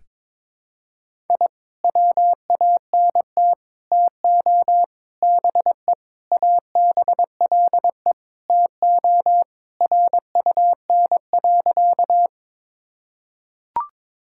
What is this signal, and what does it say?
Morse code, heard as a keyed tone, about 22 wpm, 700 Hz. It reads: I WANT TO BE ABLE TO RUN.
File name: practice / 22wpm / Sentences with 3-Word Phrases - 22wpm